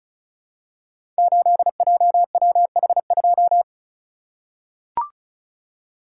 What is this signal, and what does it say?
Morse code, heard as a keyed tone, about 36 wpm, 690 Hz. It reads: 8JWH2
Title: Code Groups - 5 Letter:Numbers - 35wpm